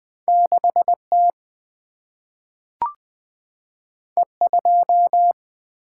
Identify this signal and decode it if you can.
Morse code, heard as a keyed tone, about 20 wpm, 700 Hz. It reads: 6T E2
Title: Single-Letter Single-Number 20wpm